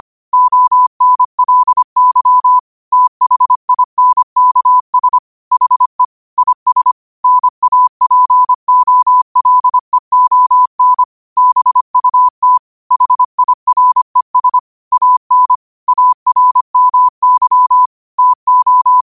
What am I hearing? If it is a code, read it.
Morse code, heard as a keyed tone, about 25 wpm, 1000 Hz. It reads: ONLY THINKS HE IS NAPOLEON BUT HIRES AN ARMY TO
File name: qotdfiles / 25wpm